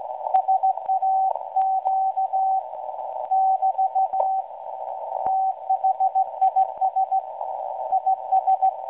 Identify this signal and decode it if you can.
Morse code, heard as a keyed tone, about 17 wpm, 760 Hz, with radio noise. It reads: SAQ <BT> THIS IS
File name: WebSDR records / SAQ at PI4THT msg 0800-0804 17.2kHz